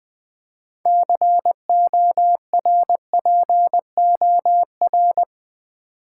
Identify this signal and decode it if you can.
Morse code, heard as a keyed tone, about 20 wpm, 700 Hz. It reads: CORPOR
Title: Build A Word - 3 Letters At A Time - 9-Letter Words - 20wpm